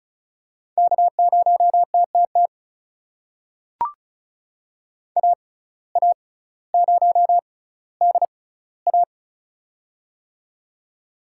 Morse code, heard as a keyed tone, about 36 wpm, spaced as for 18 wpm, 700 Hz. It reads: K0TTT AA0DA
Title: USA Call Signs - 5 Characters - 6x Character Spacing - 35wpm